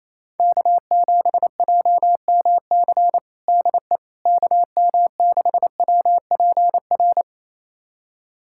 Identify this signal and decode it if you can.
Morse code, heard as a keyed tone, about 28 wpm, 700 Hz. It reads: K7JMC DE KM6WPR
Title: USA Callsign DE Callsign 28wpm